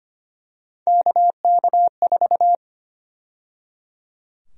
Morse code, heard as a keyed tone, about 25 wpm, 700 Hz. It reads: KK4
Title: USA Call Sign Prefixes - 25wpm